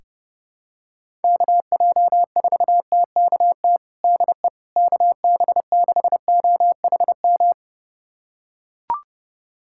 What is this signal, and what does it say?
Morse code, heard as a keyed tone, about 30 wpm, 700 Hz. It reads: KJ4TKT DE KB6OHM
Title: USA Callsign DE Callsign 30wpm